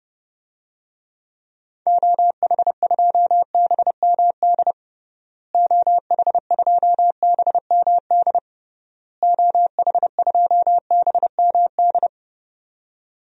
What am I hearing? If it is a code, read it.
Morse code, heard as a keyed tone, about 32 wpm, 700 Hz. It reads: OH2BMD OH2BMD OH2BMD
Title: DXCC-100 Call Signs - 6 Characters - Repeated 3x - 30wpm